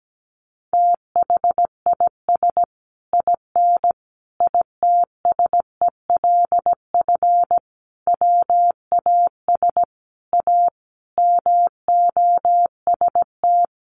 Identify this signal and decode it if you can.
Morse code, heard as a keyed tone, about 17 wpm, 700 Hz. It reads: THIS IN ITSELF WAS A MOST
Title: Sentences from Top 1400 Words 17wpm